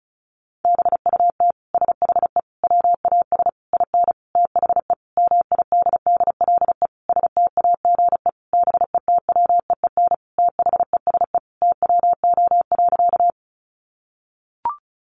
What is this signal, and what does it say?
Morse code, heard as a keyed tone, about 34 wpm, 700 Hz. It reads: BUT SHE WAS IN THE MIDDLE STAGE BETWEEN THESE TWO.
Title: Sentences from Top 1200 Words 35wpm